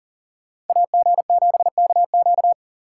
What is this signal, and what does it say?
Morse code, heard as a keyed tone, about 49 wpm, 700 Hz. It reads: AG7KQ
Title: Speed Racing - USA 2x2 Call Signs 40wpm